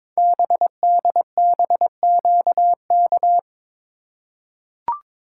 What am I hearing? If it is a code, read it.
Morse code, heard as a keyed tone, about 22 wpm, 700 Hz. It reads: BDBQK
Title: Code Groups - 5 Letters - 22wpm